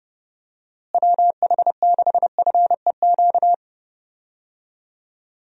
Morse code, heard as a keyed tone, about 30 wpm, 700 Hz. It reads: WH6FEQ
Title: USA Call Signs - 6 Characters - Repeated 2x - 30wpm